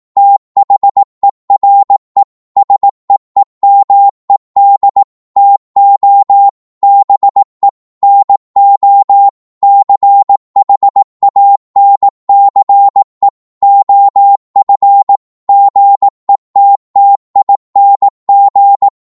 Morse code, heard as a keyed tone, about 18 wpm, 800 Hz. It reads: THERE SEEMED TO BE NO CHANCE OF GETTING
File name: 18wpm